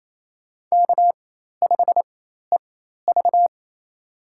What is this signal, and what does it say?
Morse code, heard as a keyed tone, about 27 wpm, spaced as for 13 wpm, 700 Hz. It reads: K5EV
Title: USA Call Signs - 4 Characters - 4x Character Spacing - 28wpm